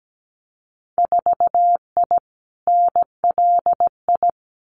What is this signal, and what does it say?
Morse code, heard as a keyed tone, about 17 wpm, 700 Hz. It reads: 4I NLI
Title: Winter Field Day Exchanges Encore 17wpm